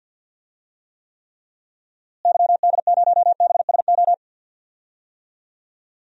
Morse code, heard as a keyed tone, about 51 wpm, 700 Hz. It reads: YD0BSO
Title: DXCC-100 Call Signs - 6 Characters - Repeated 3x - 50wpm